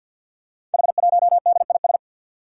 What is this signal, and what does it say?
Morse code, heard as a keyed tone, about 48 wpm, 700 Hz. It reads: H1DIS
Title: Five Challenging Letter-Number Combinations 50wpm